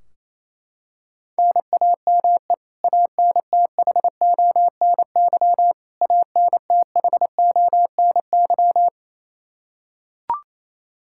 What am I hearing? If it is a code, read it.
Morse code, heard as a keyed tone, about 28 wpm, 700 Hz. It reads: NAME ANTHONY ANTHONY
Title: Practice QSO - Exchange 1 - Beginning - 28wpm